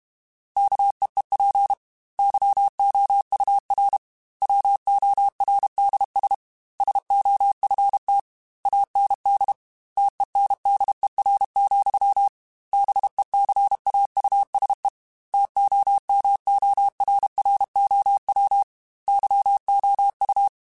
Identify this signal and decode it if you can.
Morse code, heard as a keyed tone, about 32 wpm, 800 Hz. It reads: KEEP YOUR WORDS SOFT AND TENDER, BECAUSE TOMORROW YOU
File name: Chuckles / Chuckles-mp3s / SoftWords @30